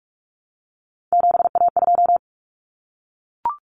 Morse code, heard as a keyed tone, about 44 wpm, 700 Hz. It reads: 7A2
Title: Common Cut Numbers - 3-Digit 45wpm